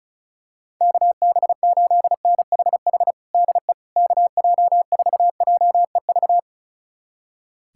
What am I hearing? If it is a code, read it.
Morse code, heard as a keyed tone, about 36 wpm, 700 Hz. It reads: KB8NHH DE KJ4JEV
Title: USA Callsign DE Callsign 35wpm